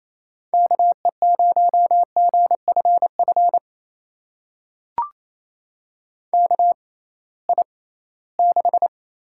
Morse code, heard as a keyed tone, about 28 wpm, spaced as for 18 wpm, 700 Hz. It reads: KE0GFF KI6
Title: USA Call Signs - 6 Characters - 6x Character Spacing - 28wpm